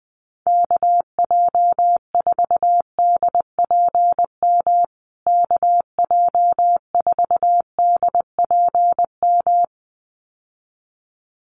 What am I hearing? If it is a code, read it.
Morse code, heard as a keyed tone, about 20 wpm, 700 Hz. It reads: KJ4DPM KJ4DPM
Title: USA Call Signs Repeated 2x 20wpm